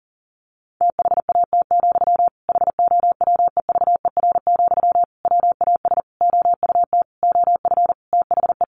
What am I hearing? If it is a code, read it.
Morse code, heard as a keyed tone, about 40 wpm, 700 Hz. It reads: THAT, HOWEVER, WAS OUT OF THE
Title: Sentences from Top 600 Words 40wpm